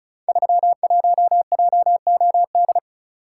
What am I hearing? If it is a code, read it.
Morse code, heard as a keyed tone, about 34 wpm, 700 Hz. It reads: 31JOD
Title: Code Groups - 5 Letter:Numbers - 35wpm